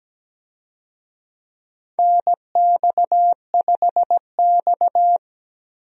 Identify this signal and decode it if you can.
Morse code, heard as a keyed tone, about 17 wpm, 700 Hz. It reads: NX5X
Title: USA Call Signs - 4 Characters - 17wpm